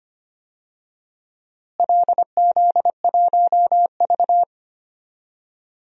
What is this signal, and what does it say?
Morse code, heard as a keyed tone, about 25 wpm, 700 Hz. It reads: LZ1V
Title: DXCC-100 Call Signs - 4 Characters - Repeated 2x - 25wpm